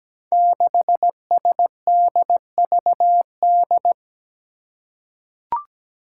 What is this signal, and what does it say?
Morse code, heard as a keyed tone, about 17 wpm, 700 Hz. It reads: 6SDVD
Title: Five Challenging Letter-Number Combinations 17wpm